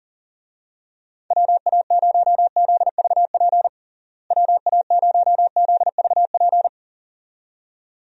Morse code, heard as a keyed tone, about 40 wpm, 700 Hz. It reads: WA0ZVP WA0ZVP
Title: USA Call Signs - 6 Characters - Repeated 2x - 40wpm